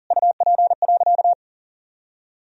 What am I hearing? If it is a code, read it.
Morse code, heard as a keyed tone, about 40 wpm, 700 Hz. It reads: UP.
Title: Sentences from Top 500 Words and Abbreviations 40wpm